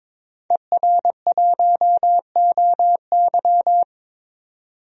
Morse code, heard as a keyed tone, about 21 wpm, 700 Hz. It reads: ER1OY
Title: DXCC-100 Call Signs - 5 Characters - 22wpm